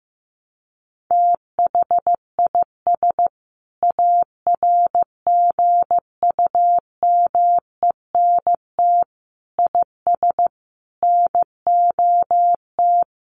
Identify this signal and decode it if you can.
Morse code, heard as a keyed tone, about 15 wpm, 700 Hz. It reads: THIS ARGUMENT IS NOT
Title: Sentences from Top 1400 Words 15wpm